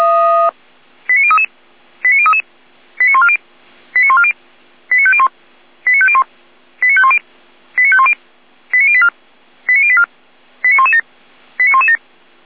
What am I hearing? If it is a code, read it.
Five-tone selective call (ZVEI), heard as quick runs of five tones, tones of about 70 ms. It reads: 89033 89033 89140 89140 89501 89501 89510 89510 89084 89084 89117 89117
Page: SDR/ZVEI